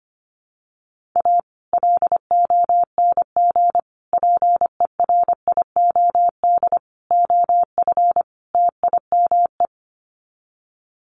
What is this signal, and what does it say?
Morse code, heard as a keyed tone, about 25 wpm, 700 Hz. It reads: A LONG PERIOD OF TIME
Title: Sets of 5 Words 25wpm